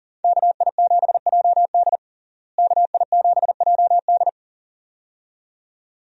Morse code, heard as a keyed tone, about 40 wpm, 700 Hz. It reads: KI7JD KI7JD
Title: USA Call Signs - 5 Characters - Repeated 2x - 40wpm